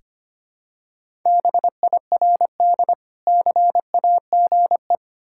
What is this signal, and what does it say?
Morse code, heard as a keyed tone, about 25 wpm, 700 Hz. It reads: BIRD CAGE
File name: practice / 25wpm / Difficult Words with Hints - 25wpm